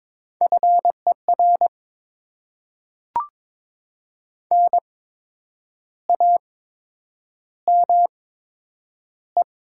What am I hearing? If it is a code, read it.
Morse code, heard as a keyed tone, about 22 wpm, spaced as for 8 wpm, 700 Hz. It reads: FER NAME
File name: Top 20 QSO Elements - 8x Character Spacing - 22wpm